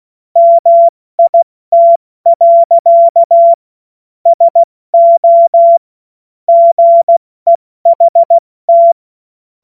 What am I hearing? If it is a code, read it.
Morse code, heard as a keyed tone, about 16 wpm, 680 Hz. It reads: MIT. SO GEHT
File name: Morsekurs / Witze / Witz 15. November 2022 70 F80